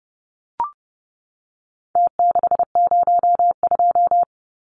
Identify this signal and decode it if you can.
Morse code, heard as a keyed tone, about 30 wpm, 700 Hz. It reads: T602
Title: Numbers - Contest Style - 4-Digit 30wpm